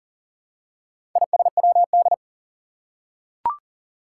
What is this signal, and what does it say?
Morse code, heard as a keyed tone, about 39 wpm, 700 Hz. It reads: ISWD
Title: Four Challenging Letter-Number Combinations 40wpm